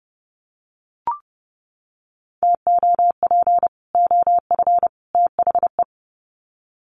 Morse code, heard as a keyed tone, about 30 wpm, 700 Hz. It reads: TOP OF THE
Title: Sets of 3 Words 30wpm